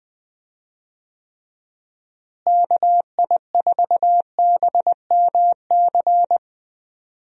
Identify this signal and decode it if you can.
Morse code, heard as a keyed tone, about 20 wpm, 700 Hz. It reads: KI4BMC
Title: USA Call Signs - 6 Characters - Repeated 3x - 20wpm